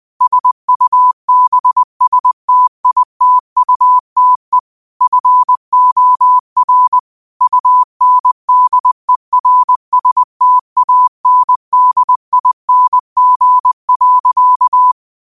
Morse code, heard as a keyed tone, about 20 wpm, 1000 Hz. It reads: SUBSTITUTE FOR UNDERSTANDING.
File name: qotdfiles / 20wpm